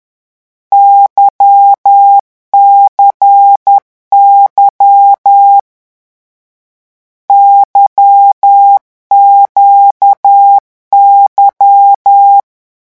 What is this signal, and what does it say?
Morse code, heard as a keyed tone, about 10 wpm, 800 Hz. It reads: YCY YQY
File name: morse mp3 / lecon2-vitesse 10